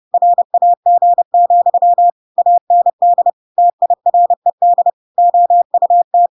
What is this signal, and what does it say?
Morse code, heard as a keyed tone, about 30 wpm, 700 Hz. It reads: RAG, AND TIRED OUT
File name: Pinocchio / Pinocchio ch6 30 30 ik1hge